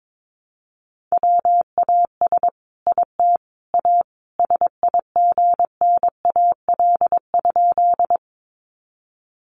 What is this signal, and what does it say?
Morse code, heard as a keyed tone, about 22 wpm, 700 Hz. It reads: WAS IT A SIGNAL?
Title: Sentences from Top 1400 Words 22wpm